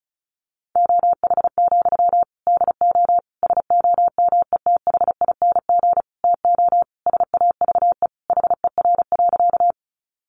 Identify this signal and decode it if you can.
Morse code, heard as a keyed tone, about 34 wpm, 700 Hz. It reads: OH, DO SOMETHING TO SAVE HER.
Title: Sentences from Top 1100 Words 35wpm